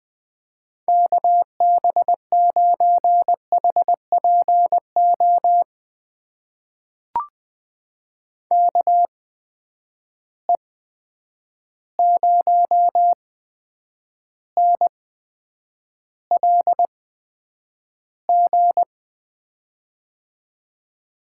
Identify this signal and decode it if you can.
Morse code, heard as a keyed tone, about 20 wpm, spaced as for 9 wpm, 700 Hz. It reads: KB9HPO KE0NLG
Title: USA Call Signs - 6 Characters - 8x Character Spacing - 20wpm